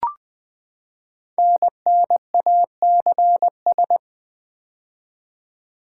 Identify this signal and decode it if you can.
Morse code, heard as a keyed tone, about 21 wpm, 700 Hz. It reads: NNACS
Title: Code Groups - 5 Letters - 20wpm